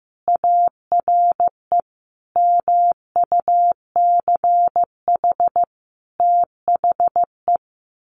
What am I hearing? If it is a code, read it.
Morse code, heard as a keyed tone, about 15 wpm, 700 Hz. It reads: ARE MUCH THE
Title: Sentences from Top 1100 Words 15wpm